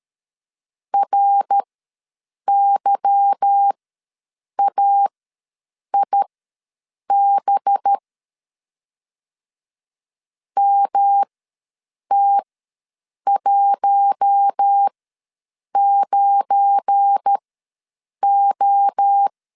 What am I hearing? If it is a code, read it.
Morse code, heard as a keyed tone, about 13 wpm, spaced as for 7 wpm, 800 Hz. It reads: RYAIB MT19O
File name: LEZIONI / 8 YJ45 / 8 Y ws 12wpm 3409